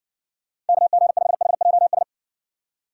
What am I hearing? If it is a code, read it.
Morse code, heard as a keyed tone, about 70 wpm, 700 Hz. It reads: KG5HJS